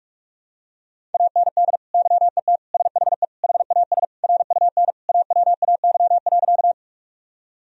Morse code, heard as a keyed tone, about 41 wpm, 700 Hz. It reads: AND YET SHE HAS RUN AWAY.
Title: Sentences from Top 1000 Words - Review Part 1 - 45wpm